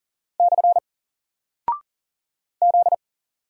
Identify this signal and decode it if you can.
Morse code, heard as a keyed tone, about 38 wpm, 700 Hz. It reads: / Z
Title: Single Character (Letter:Number:Punctuation:Prosign) Alt 40wpm